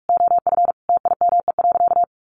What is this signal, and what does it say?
Morse code, heard as a keyed tone, about 46 wpm, 700 Hz. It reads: OF TIME.
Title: Sentences from Top 1300 Words 45wpm